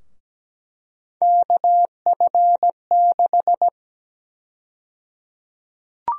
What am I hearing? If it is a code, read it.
Morse code, heard as a keyed tone, about 17 wpm, 700 Hz. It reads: KF6
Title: USA Call Sign Prefixes - 17wpm